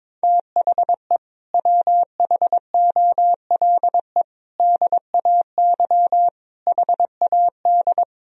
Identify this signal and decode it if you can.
Morse code, heard as a keyed tone, about 22 wpm, 700 Hz. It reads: THE WHOLE DAY HAD
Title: Sentences from Top 500 Words and Abbreviations 22wpm